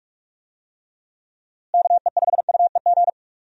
Morse code, heard as a keyed tone, about 46 wpm, 700 Hz. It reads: KE5UEG
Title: USA Call Signs - 6 Characters - Repeated 2x - 45wpm